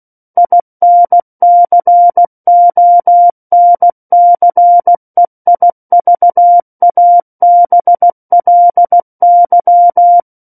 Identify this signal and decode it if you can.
Morse code, heard as a keyed tone, about 16 wpm, 700 Hz. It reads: INCONCEIVABLY
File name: Book One / 07-Chapter 7 - 16 WPM